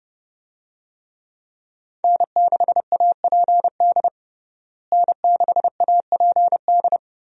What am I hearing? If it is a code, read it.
Morse code, heard as a keyed tone, about 32 wpm, 700 Hz. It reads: N6APD N6APD
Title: USA Call Signs - 5 Characters - Repeated 2x - 30wpm